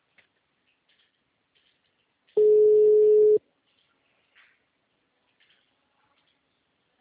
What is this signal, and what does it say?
Ringback tone (Europe, 425 Hz).